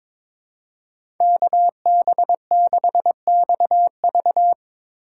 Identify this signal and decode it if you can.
Morse code, heard as a keyed tone, about 22 wpm, 700 Hz. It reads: KB6XV
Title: USA Call Signs - 5 Characters - 4x Character Spacing - 22wpm